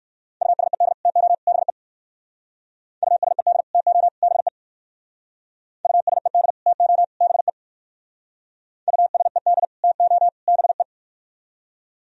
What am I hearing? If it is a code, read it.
Morse code, heard as a keyed tone, about 58 wpm, 700 Hz. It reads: USED TO BE USED TO BE USED TO BE USED TO BE